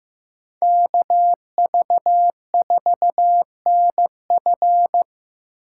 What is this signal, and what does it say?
Morse code, heard as a keyed tone, about 15 wpm, 700 Hz. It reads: KV4NF
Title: USA Call Signs - 5 Characters - 3x Character Spacing - 15wpm